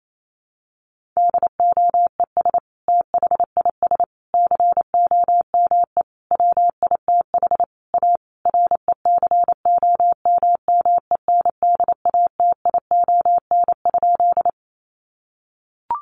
Morse code, heard as a keyed tone, about 28 wpm, 700 Hz. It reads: DOES THIS COME WITH A RECOMMENDATION?
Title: Sentences from Top 1400 Words 28wpm